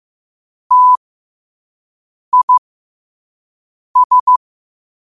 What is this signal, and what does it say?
Morse code, heard as a keyed tone, about 15 wpm, spaced as for 5 wpm, 1000 Hz. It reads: TIS
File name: qotdfiles / 5wpm